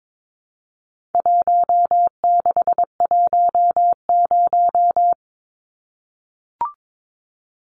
Morse code, heard as a keyed tone, about 22 wpm, 700 Hz. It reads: 1610
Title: Numbers - 4-Digit 22wpm